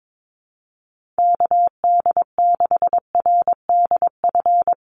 Morse code, heard as a keyed tone, about 22 wpm, 700 Hz. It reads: KD6RDF
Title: USA Call Signs Repeated 3x 22wpm